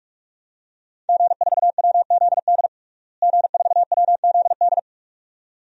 Morse code, heard as a keyed tone, about 46 wpm, 700 Hz. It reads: G4WZD G4WZD
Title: DXCC-100 Call Signs - 5 Characters - Repeated 2x - 45wpm